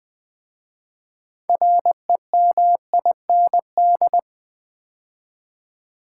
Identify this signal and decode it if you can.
Morse code, heard as a keyed tone, about 20 wpm, 700 Hz. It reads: REMIND
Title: 6 Letter Words - First 3 Letters Given - 20wpm